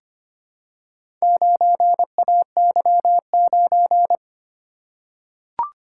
Morse code, heard as a keyed tone, about 25 wpm, 700 Hz. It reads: 9AY9
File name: practice / 25wpm / Four Letter-Number Combinations - 25wpm